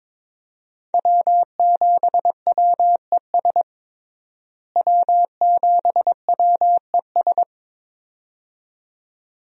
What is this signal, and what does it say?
Morse code, heard as a keyed tone, about 22 wpm, 700 Hz. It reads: W7WES W7WES
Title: USA Call Signs - 5 Characters - Repeated 2x - 22wpm